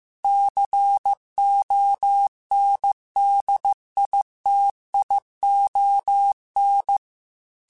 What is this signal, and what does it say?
Morse code, heard as a keyed tone, about 15 wpm, 800 Hz. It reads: CONDITION
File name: Technical Subjects / Technical Subjects-mp3s / AND Gate @13